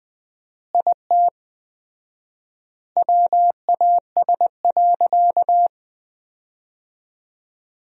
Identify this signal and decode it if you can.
Morse code, heard as a keyed tone, about 20 wpm, 700 Hz. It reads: IT WAS.
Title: 4x Word-Spacing - Sentences from Top 100 Words - 20wpm